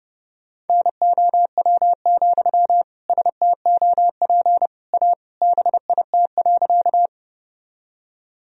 Morse code, heard as a keyed tone, about 32 wpm, 700 Hz. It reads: NOW, STOP A BIT.
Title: Sentences from Top 1000 Words - Review Part 1 - 30wpm